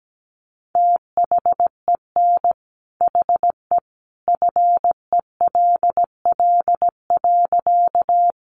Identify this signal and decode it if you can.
Morse code, heard as a keyed tone, about 17 wpm, 700 Hz. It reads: THEN HE FELL.
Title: A Sound Of Thunder 17wpm